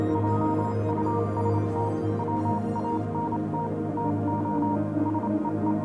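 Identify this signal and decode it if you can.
Morse code, heard as a keyed tone, about 27 wpm, 940 Hz, with noise. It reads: JUST WANT TO SEE
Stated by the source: Morse code version of a fan-made pony music track.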